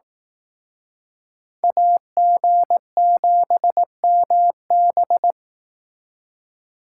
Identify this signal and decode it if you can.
Morse code, heard as a keyed tone, about 18 wpm, 700 Hz. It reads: AG7MB
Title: Speed Racing - USA 2x2 Call Signs 15wpm